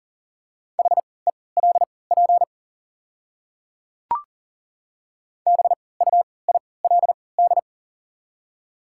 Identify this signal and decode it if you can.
Morse code, heard as a keyed tone, about 40 wpm, spaced as for 23 wpm, 690 Hz. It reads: HELP BUILD
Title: 3x Character Spacing - Top 200 Words - 40wpm